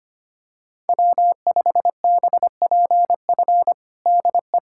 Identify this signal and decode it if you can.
Morse code, heard as a keyed tone, about 25 wpm, 700 Hz. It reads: W5BPF DE
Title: USA Callsign DE Callsign 25wpm